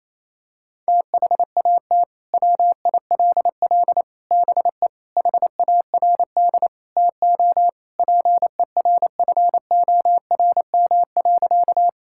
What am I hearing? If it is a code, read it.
Morse code, heard as a keyed tone, about 28 wpm, 700 Hz. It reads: THAT WILL BE HARD TO PERFORM.
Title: Sentences from Top 1000 Words - Review Part 5 - 28wpm